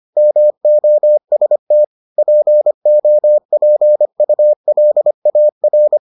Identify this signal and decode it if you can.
Morse code, heard as a keyed tone, about 25 wpm, 600 Hz. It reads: MOST POPULAR